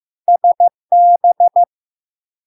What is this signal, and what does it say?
Morse code, heard as a keyed tone, about 15 wpm, 700 Hz. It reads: SB
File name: LARK-FD-2022-Log-in-CW-15wpm-700-Hz